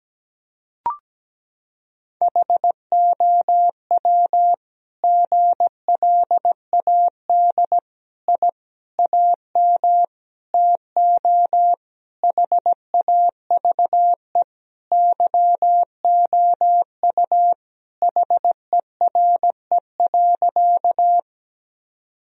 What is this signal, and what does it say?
Morse code, heard as a keyed tone, about 17 wpm, 700 Hz. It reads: HOW GLAD I AM TO HAVE YOU HERE.
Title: Sentences from Top 1000 Words - Review Part 2 - 17wpm